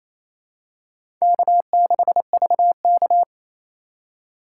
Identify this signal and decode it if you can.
Morse code, heard as a keyed tone, about 28 wpm, 700 Hz. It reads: K6VK
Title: USA Call Signs - 4 Characters - 3x Character Spacing - 28wpm